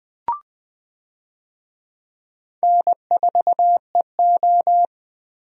Morse code, heard as a keyed tone, about 20 wpm, 700 Hz. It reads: N4EO
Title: USA Call Signs - 4 Characters - Repeated 2x - 20wpm